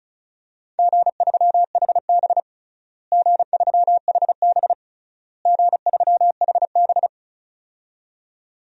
Morse code, heard as a keyed tone, about 34 wpm, 700 Hz. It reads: G3HB G3HB G3HB
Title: DXCC-100 Call Signs - 4 Characters - Repeated 3x - 35wpm